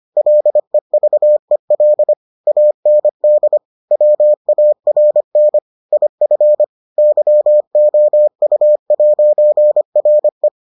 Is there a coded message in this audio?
Yes: Morse code from 0.2 to 10.5 s, heard as a keyed tone, about 25 wpm, 600 Hz. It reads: LEVEL AND WARN IF YOU'RE